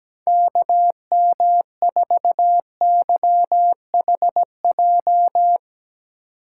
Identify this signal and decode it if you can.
Morse code, heard as a keyed tone, about 17 wpm, 700 Hz. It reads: KM4YHJ
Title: USA Call Signs - 6 Characters - 8x Character Spacing - 17wpm